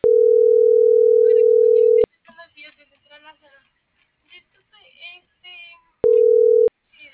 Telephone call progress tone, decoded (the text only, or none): ringback tone